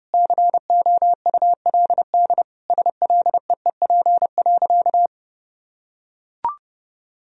Morse code, heard as a keyed tone, about 30 wpm, 700 Hz. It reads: COULD SLEEP.
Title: Sentences from Top 500 Words and Abbreviations 30wpm